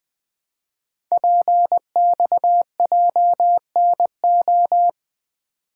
Morse code, heard as a keyed tone, about 20 wpm, 700 Hz. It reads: PXJNO
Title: Code Groups - 5 Letters - 20wpm